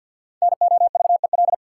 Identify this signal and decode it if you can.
Morse code, heard as a keyed tone, about 53 wpm, 700 Hz. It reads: NOVEL
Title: Sentences from Top 1300 Words - New Word Drill Alt - 50wpm